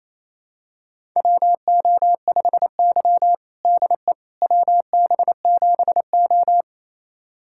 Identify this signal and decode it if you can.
Morse code, heard as a keyed tone, about 28 wpm, 700 Hz. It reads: WO5Y DE WB7O
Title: USA Callsign DE Callsign - 4 Characters - 28wpm